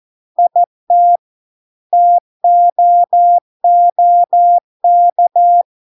Morse code, heard as a keyed tone, about 14 wpm, 700 Hz. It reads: IT TOOK